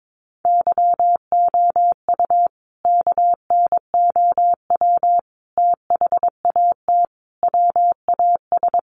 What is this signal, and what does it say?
Morse code, heard as a keyed tone, about 22 wpm, 700 Hz. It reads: YOU KNOW THAT WAS
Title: Sentences from Top 1500 Words 22wpm